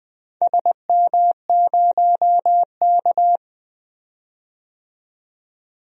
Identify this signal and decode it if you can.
Morse code, heard as a keyed tone, about 20 wpm, 700 Hz. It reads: SM0K
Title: DXCC-100 Call Signs - 4 Characters - 20wpm